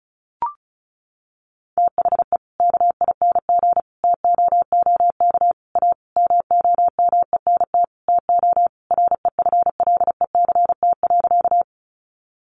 Morse code, heard as a keyed tone, about 34 wpm, 700 Hz. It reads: THE KING TOOK A MOMENT TO REFLECT.
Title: Sentences from Top 1500 Words 35wpm